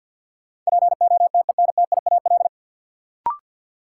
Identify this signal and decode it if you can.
Morse code, heard as a keyed tone, about 52 wpm, 700 Hz. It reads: POTENTIAL
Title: Sentences from Top 1000 Words - Review Part 4 - New Word Drill - 50wpm